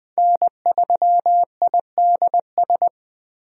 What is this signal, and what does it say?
Morse code, heard as a keyed tone, about 20 wpm, 700 Hz. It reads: N3IDS
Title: USA Call Signs - 5 Characters - 20wpm